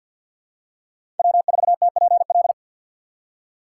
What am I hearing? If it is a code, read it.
Morse code, heard as a keyed tone, about 53 wpm, 700 Hz. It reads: W4TPL
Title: USA Call Signs - 5 Characters - 4x Character Spacing - 50wpm